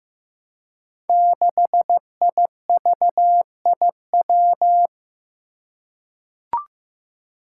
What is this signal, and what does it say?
Morse code, heard as a keyed tone, about 15 wpm, 700 Hz. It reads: 6IVIW
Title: Five Challenging Letter-Number Combinations 15wpm